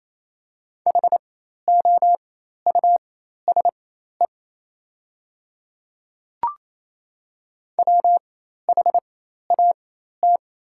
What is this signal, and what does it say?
Morse code, heard as a keyed tone, about 28 wpm, spaced as for 13 wpm, 690 Hz. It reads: HOUSE WHAT
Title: 4x Character Spacing - Top 200 Words - 28wpm